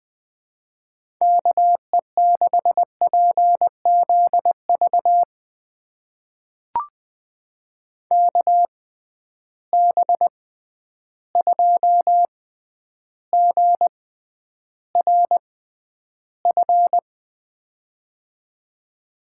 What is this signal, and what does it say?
Morse code, heard as a keyed tone, about 21 wpm, spaced as for 10 wpm, 700 Hz. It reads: KE6PZV KB2GRF
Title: USA Call Signs - 6 Characters - 6x Character Spacing - 20wpm